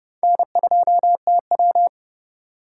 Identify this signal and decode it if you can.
Morse code, heard as a keyed tone, about 37 wpm, 700 Hz. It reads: N2TW
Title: Speed Racing - USA 1x2 Call Signs 30wpm